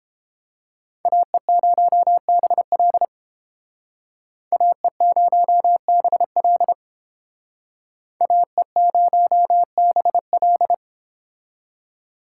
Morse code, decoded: AE0BL AE0BL AE0BL

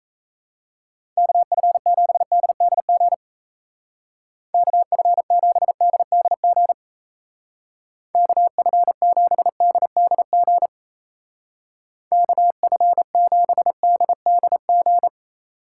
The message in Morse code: KF7DDG KF7DDG KF7DDG KF7DDG

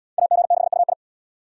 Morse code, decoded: AG6UI

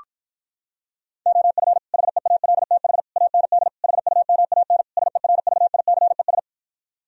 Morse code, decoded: OF HEALTH AND HUMAN SERVICES

